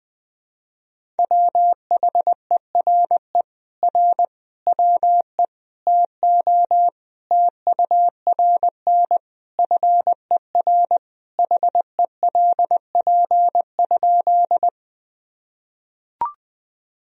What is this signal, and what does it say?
Morse code, heard as a keyed tone, about 19 wpm, 700 Hz. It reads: WHERE R WE TO TURN FER HELP?
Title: Sentences from Top 200 Words and Abbreviations 20wpm